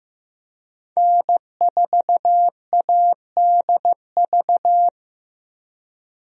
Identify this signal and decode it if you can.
Morse code, heard as a keyed tone, about 15 wpm, 700 Hz. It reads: N4ADV
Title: USA Call Signs - 5 Characters - Repeated 3x - 15wpm